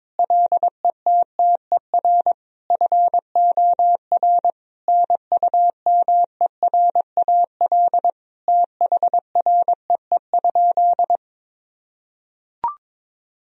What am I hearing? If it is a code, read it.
Morse code, heard as a keyed tone, about 22 wpm, 700 Hz. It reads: LETTER FOR NUMERAL THREE?
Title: Sentences from Top 1000 Words - Review Part 2 - 22wpm